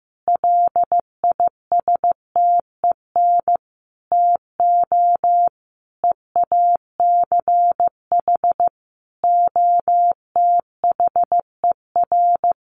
Morse code, heard as a keyed tone, about 15 wpm, 700 Hz. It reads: LISTEN TO EACH OTHER
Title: Sentences from Top 1300 Words 15wpm